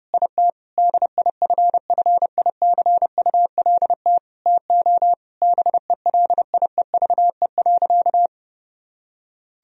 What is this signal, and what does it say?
Morse code, heard as a keyed tone, about 30 wpm, 700 Hz. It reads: IT DIFFICULT TO BELIEVE.
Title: Sentences from Top 1000 Words - Review Part 4 - 30wpm